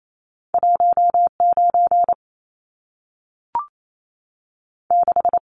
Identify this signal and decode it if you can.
Morse code, heard as a keyed tone, about 28 wpm, 700 Hz. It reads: 1 9 6
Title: Numbers - 2-Digit 28wpm